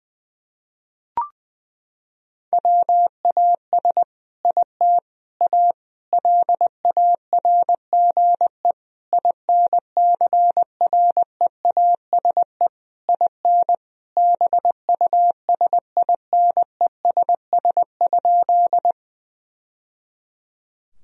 Morse code, heard as a keyed tone, about 20 wpm, 700 Hz. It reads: WAS IT A LARGE INCREASE IN BUSINESS?
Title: Sentences from Top 1000 Words - Review Part 4 - 20wpm